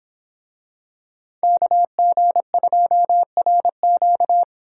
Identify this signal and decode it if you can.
Morse code, heard as a keyed tone, about 24 wpm, 700 Hz. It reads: KG2RQ